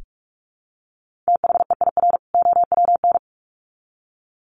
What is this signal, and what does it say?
Morse code, heard as a keyed tone, about 44 wpm, 700 Hz. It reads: THEIR OWN
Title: Sets of 2 Words 45wpm